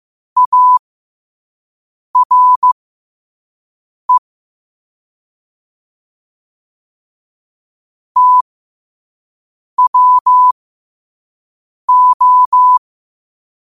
Morse code, heard as a keyed tone, about 15 wpm, spaced as for 5 wpm, 1000 Hz. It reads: ARE TWO
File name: qotdfiles / 5wpm